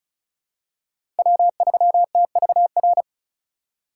Morse code, heard as a keyed tone, about 36 wpm, 700 Hz. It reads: W3TVR
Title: USA Call Signs - 2x Character Spacing - 35wpm